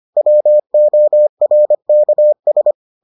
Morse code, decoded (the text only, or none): WORKS